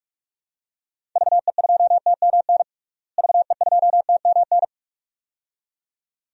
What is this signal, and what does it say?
Morse code, heard as a keyed tone, about 42 wpm, 700 Hz. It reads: VE2TMN VE2TMN
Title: DXCC-100 Call Signs - 6 Characters - Repeated 2x - 45wpm